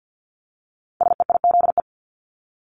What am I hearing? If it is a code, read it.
Morse code, heard as a keyed tone, about 50 wpm, 700 Hz. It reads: SEIZE
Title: Multi-letter Phonemes Repeated 3x 50wpm